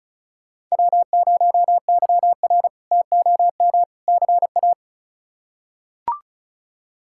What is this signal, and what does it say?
Morse code, heard as a keyed tone, about 36 wpm, 700 Hz. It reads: W0YR TOM CA
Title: K1USN SST QSO Practice - 35wpm